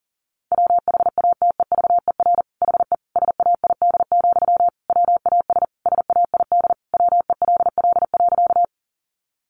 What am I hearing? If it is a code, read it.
Morse code, heard as a keyed tone, about 37 wpm, 700 Hz. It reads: WHATEVER HE SAID, WAS SAID WELL.
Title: Sentences from Top 1100 Words 40wpm